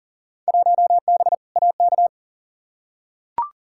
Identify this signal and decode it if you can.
Morse code, heard as a keyed tone, about 40 wpm, 700 Hz. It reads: 1B AK
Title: Field Day Exchanges Encore - 40wpm